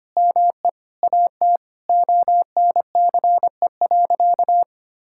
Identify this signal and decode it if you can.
Morse code, heard as a keyed tone, about 25 wpm, 700 Hz. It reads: ME AT ONCE.
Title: Sentences from Top 1000 Words - Review Part 6 - 25wpm